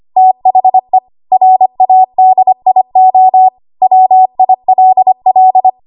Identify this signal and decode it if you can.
Morse code, heard as a keyed tone, about 25 wpm, 750 Hz. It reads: THE RADIO WILL
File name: Archive / 25 WPM / 25WPM